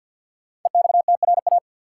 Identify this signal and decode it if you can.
Morse code, heard as a keyed tone, about 52 wpm, 700 Hz. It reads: EXTRA
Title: Sentences from Top 1000 Words - New Word Drill - 50wpm